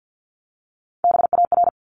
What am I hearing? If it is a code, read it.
Morse code, heard as a keyed tone, about 50 wpm, 700 Hz. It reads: BAR
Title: Top 1751-2000 Words 50wpm